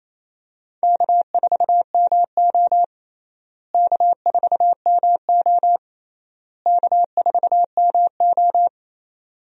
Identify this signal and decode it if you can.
Morse code, heard as a keyed tone, about 28 wpm, 700 Hz. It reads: K4MO K4MO K4MO
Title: USA Call Signs - 4 Characters - Repeated 3x - 28wpm